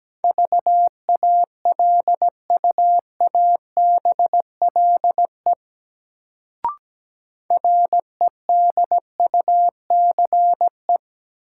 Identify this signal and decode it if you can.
Morse code, heard as a keyed tone, about 17 wpm, 700 Hz. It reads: VALUABLE REDUCE
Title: Sentences from Top 1100 Words - New Word Drill - 17wpm